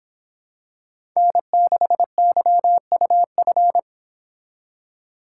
Morse code, heard as a keyed tone, about 25 wpm, 700 Hz. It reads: N6YUF